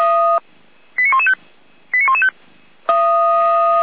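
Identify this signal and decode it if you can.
Five-tone selective call (ZVEI), heard as quick runs of five tones, tones of about 70 ms. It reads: 89115 89115
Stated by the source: ZVEI five-tone selective call received with an SDR.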